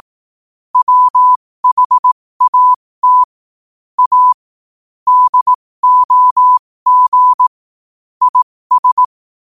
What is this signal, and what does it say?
Morse code, heard as a keyed tone, about 18 wpm, spaced as for 15 wpm, 1000 Hz. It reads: WHAT A DOG IS
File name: qotdfiles / 15wpm